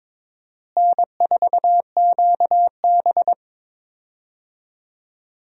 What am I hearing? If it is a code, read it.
Morse code, heard as a keyed tone, about 22 wpm, 700 Hz. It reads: N4QB
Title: USA Call Signs - 4 Characters - Repeated 2x - 22wpm